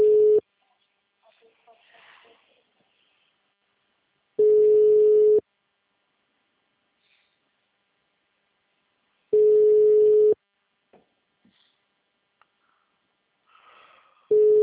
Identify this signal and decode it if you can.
Ringback tone (Europe, 425 Hz), with speech.